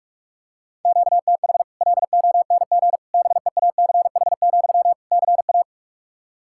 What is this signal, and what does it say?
Morse code, heard as a keyed tone, about 42 wpm, 700 Hz. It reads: QTH LONG BEACH, CA
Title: Practice QSO - Exchange 1 - Beginning - 45wpm